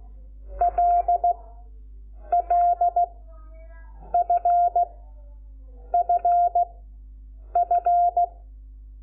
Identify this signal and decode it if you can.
Morse code, heard as a keyed tone, about 16 wpm, spaced as for 7 wpm, 700 Hz. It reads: LLFFF
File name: Lecciones / CLASE Nº5